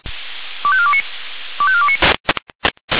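Five-tone selective call (ZVEI), heard as quick runs of five tones, tones of about 70 ms. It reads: 25429 25429
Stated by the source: Five-tone ZVEI selective call, code 25425, received by SDR.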